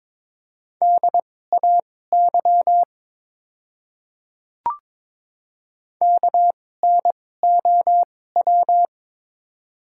Morse code, heard as a keyed tone, about 22 wpm, spaced as for 16 wpm, 700 Hz. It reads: DAY KNOW
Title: 2x Character Spacing - Top 100 Words - 22wpm